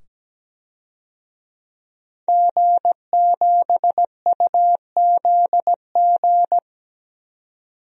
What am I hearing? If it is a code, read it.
Morse code, heard as a keyed tone, about 17 wpm, 700 Hz. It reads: G7UZG
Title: DXCC-100 Call Signs - 5 Characters - 17wpm